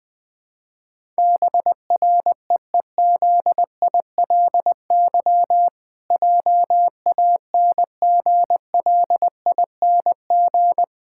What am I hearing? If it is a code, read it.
Morse code, heard as a keyed tone, about 20 wpm, 700 Hz. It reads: BREEZILY JANGLING
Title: Pangram Challenge v2 - 20wpm